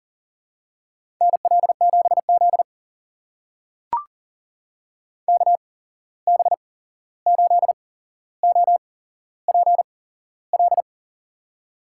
Morse code, heard as a keyed tone, about 40 wpm, spaced as for 15 wpm, 700 Hz. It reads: NL7Z KB8OPL
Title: USA Call Signs - 8x Character Spacing - 40wpm